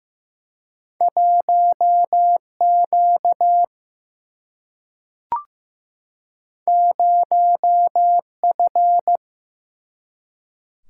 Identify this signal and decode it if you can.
Morse code, heard as a keyed tone, about 15 wpm, 700 Hz. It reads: 1 Q 0 F
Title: Single-Letter Single-Number 15wpm